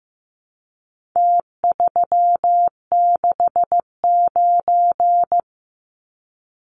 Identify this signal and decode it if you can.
Morse code, heard as a keyed tone, about 15 wpm, 700 Hz. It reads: T369
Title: Numbers - Contest Style - 4-Digit 15wpm